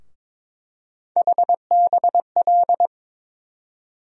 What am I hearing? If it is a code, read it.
Morse code, heard as a keyed tone, about 21 wpm, 700 Hz. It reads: HBL
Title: USA Call Sign Postfixes - 3 Letters 22wpm